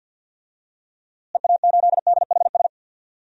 Morse code, heard as a keyed tone, about 50 wpm, 700 Hz. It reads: EA8DHS
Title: DXCC-100 Call Signs - 6 Characters - Repeated 2x - 50wpm